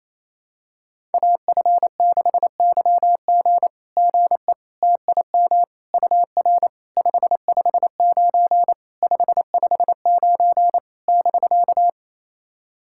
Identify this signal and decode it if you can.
Morse code, heard as a keyed tone, about 28 wpm, 690 Hz. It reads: AF6YG GE TIM UR 559 559 <BK>
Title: SOTA QSO Practice 28wpm